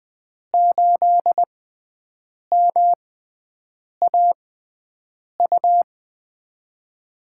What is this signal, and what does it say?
Morse code, heard as a keyed tone, about 20 wpm, spaced as for 7 wpm, 700 Hz. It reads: 8MAU
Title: USA Call Signs - 5 Characters - 6x Character Spacing - 20wpm